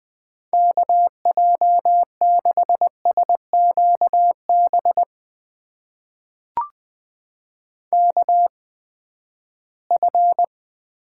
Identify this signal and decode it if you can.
Morse code, heard as a keyed tone, about 20 wpm, spaced as for 14 wpm, 700 Hz. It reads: KJ6SQB KF
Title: USA Call Signs - 8x Character Spacing - 20wpm